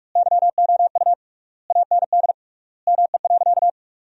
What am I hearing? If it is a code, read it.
Morse code, heard as a keyed tone, about 48 wpm, 700 Hz. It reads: YOU AND ME.
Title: 3x Word-Spacing - Sentences from Top 500 Words - 45wpm